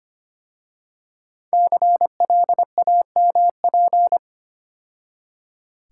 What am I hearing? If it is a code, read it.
Morse code, heard as a keyed tone, about 30 wpm, 700 Hz. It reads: CLAMP
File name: practice / 25wpm / Difficult Words - Speed-Racing with Hints - 25wpm